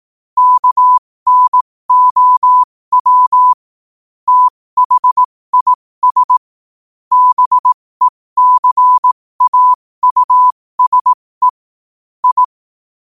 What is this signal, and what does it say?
Morse code, heard as a keyed tone, about 18 wpm, spaced as for 15 wpm, 1000 Hz. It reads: KNOW THIS BECAUSE I
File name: qotdfiles / 15wpm